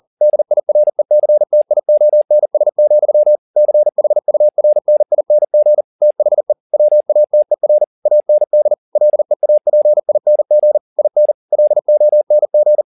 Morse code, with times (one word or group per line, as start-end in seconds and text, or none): 0.2-3.4 DIRECTIONS,
3.6-5.8 CHURNING
6.0-6.5 THE
6.7-7.8 WATER
8.1-8.7 AND
9.0-10.8 LEAPING
11.0-11.3 IN
11.5-12.8 LONG